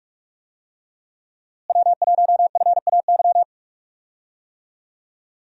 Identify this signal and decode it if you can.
Morse code, heard as a keyed tone, about 46 wpm, 700 Hz. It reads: W1FAY
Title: USA Call Signs - 5 Characters - Repeated 2x - 45wpm